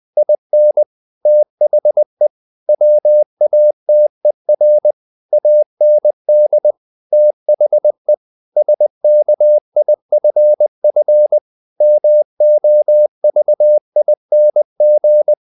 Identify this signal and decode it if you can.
Morse code, heard as a keyed tone, about 20 wpm, 600 Hz. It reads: IN THE WATER AND THE SKIFF MOVING
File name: the old man and the sea / the old man and the sea part 1 20wpm